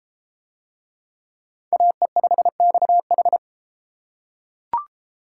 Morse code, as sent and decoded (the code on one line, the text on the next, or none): .- . ..... -..- ....
AE5XH